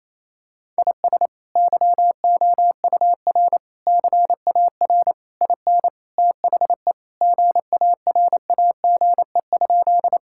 Morse code, heard as a keyed tone, about 28 wpm, 700 Hz. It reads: IS YOUR CAR IN THE GARAGE?
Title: Sentences from Top 1600 Words 28wpm